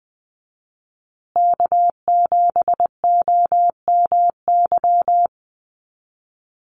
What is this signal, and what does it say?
Morse code, heard as a keyed tone, about 20 wpm, 700 Hz. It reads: K7OMY
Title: USA Call Signs Repeated 3x 20wpm